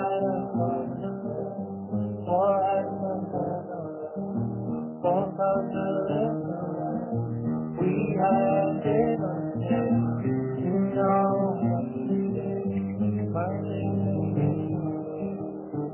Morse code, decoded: none